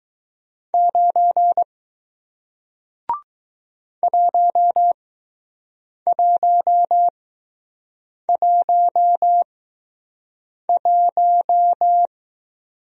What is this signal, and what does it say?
Morse code, heard as a keyed tone, about 21 wpm, spaced as for 6 wpm, 700 Hz. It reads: 9 1111